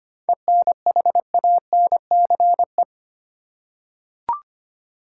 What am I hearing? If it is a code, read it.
Morse code, heard as a keyed tone, about 25 wpm, 700 Hz. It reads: ENHANCE
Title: Seven Letter Words - 25wpm